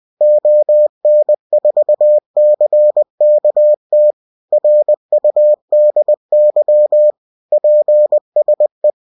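Morse code, heard as a keyed tone, about 20 wpm, 600 Hz. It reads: ON4CKT RUDY PSE
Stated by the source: Amateur station CQ call in Morse.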